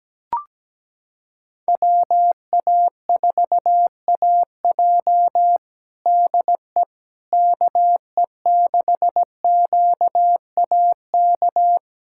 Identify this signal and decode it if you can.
Morse code, heard as a keyed tone, about 17 wpm, 700 Hz. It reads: WA4AJ DE KE6QAK
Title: USA Callsign DE Callsign 17wpm